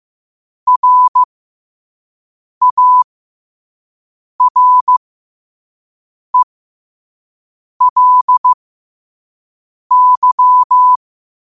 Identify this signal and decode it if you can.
Morse code, heard as a keyed tone, about 15 wpm, spaced as for 5 wpm, 1000 Hz. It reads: RARELY
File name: qotdfiles / 5wpm